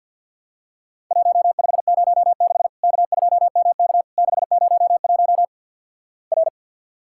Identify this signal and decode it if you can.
Morse code, heard as a keyed tone, about 50 wpm, 700 Hz. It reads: 1506 K2MK 601 R
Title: Worked All Europe DX Contest - QTC Exchange - 50wpm